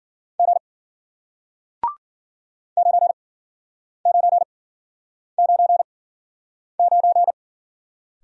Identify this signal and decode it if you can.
Morse code, heard as a keyed tone, about 53 wpm, spaced as for 9 wpm, 700 Hz. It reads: G 9999